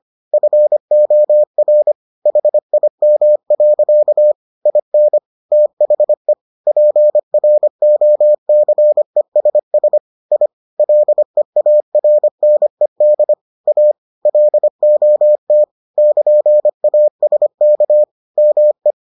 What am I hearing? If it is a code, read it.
Morse code, heard as a keyed tone, about 25 wpm, 600 Hz. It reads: FOR HIM. IN THE PROCESS I LEARNED A LOT <KN>ASK ME